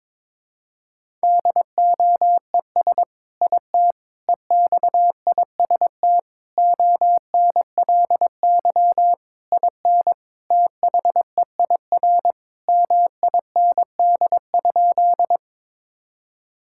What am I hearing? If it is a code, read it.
Morse code, heard as a keyed tone, about 21 wpm, 700 Hz. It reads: DOES IT EXIST ONLY IN THEIR MIND?
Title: Sentences from Top 1000 Words - Review Part 5 - 22wpm